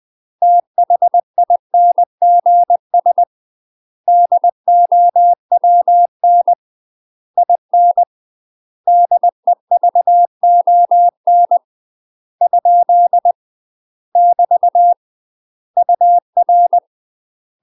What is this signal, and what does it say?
Morse code, heard as a keyed tone, about 20 wpm, 700 Hz. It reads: THINGS DOWN IN DEVON ? <BT> UR
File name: UK MASTER QSOs TRAINING / 20WPM / UK MASTER QSOs 0030